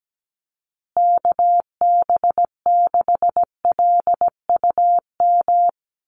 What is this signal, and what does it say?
Morse code, heard as a keyed tone, about 17 wpm, 700 Hz. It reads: KB6LUM
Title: USA Call Signs Repeated 2x 17wpm